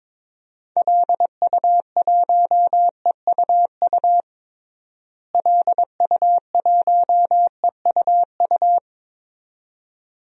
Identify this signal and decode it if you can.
Morse code, heard as a keyed tone, about 22 wpm, 700 Hz. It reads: LU1EUU LU1EUU
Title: DXCC-100 Call Signs - 6 Characters - Repeated 2x - 22wpm